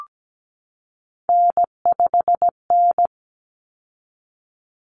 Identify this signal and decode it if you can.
Morse code, heard as a keyed tone, about 17 wpm, 700 Hz. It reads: N5N
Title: Common Cut Numbers - 3-Digit 17wpm